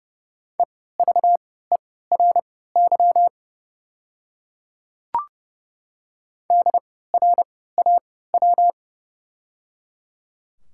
Morse code, heard as a keyed tone, about 30 wpm, spaced as for 17 wpm, 700 Hz. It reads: EVERY DRAW
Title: 3x Character Spacing - Top 300 Words - 30wpm